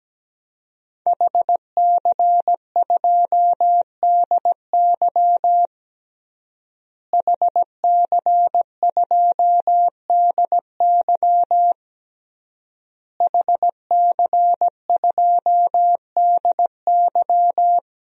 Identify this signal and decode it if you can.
Morse code, heard as a keyed tone, about 17 wpm, 700 Hz. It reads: HC2DY HC2DY HC2DY
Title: DXCC-100 Call Signs - 5 Characters - Repeated 3x - 17wpm